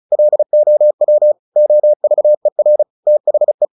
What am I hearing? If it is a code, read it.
Morse code, heard as a keyed tone, about 34 wpm, 600 Hz. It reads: LOW OVER THE
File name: the old man and the sea / the old man and the sea 1 35wpm